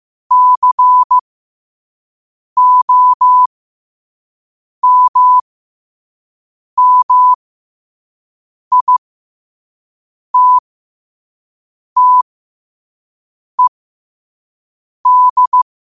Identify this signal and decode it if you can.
Morse code, heard as a keyed tone, about 15 wpm, spaced as for 5 wpm, 1000 Hz. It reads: COMMITTED
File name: qotdfiles / 5wpm